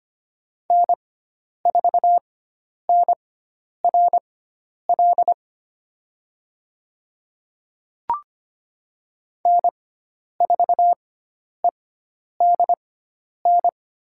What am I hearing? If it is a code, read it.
Morse code, heard as a keyed tone, about 25 wpm, spaced as for 10 wpm, 700 Hz. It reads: N4NRL N4EDN
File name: Lesson 16g - Letter c - Call Signs 25wpm